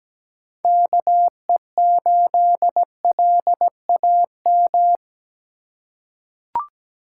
Morse code, heard as a keyed tone, about 17 wpm, 700 Hz. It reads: KE8LAM
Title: USA Call Signs - 3x Character Spacing - 17wpm